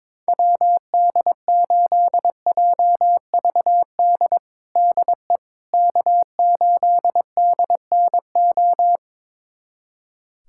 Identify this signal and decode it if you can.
Morse code, heard as a keyed tone, about 22 wpm, 700 Hz. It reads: WD8JVD DE K8DNO
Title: USA Callsign DE Callsign 22wpm